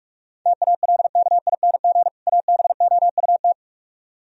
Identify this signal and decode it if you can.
Morse code, heard as a keyed tone, about 67 wpm, spaced as for 55 wpm, 690 Hz. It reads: TALKING ABOUT
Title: Speed-Racing - Sets of 2 Words 45wpm